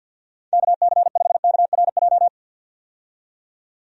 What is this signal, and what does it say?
Morse code, heard as a keyed tone, about 50 wpm, 700 Hz. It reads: KC5KRJ
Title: USA Call Signs - 6 Characters - 50wpm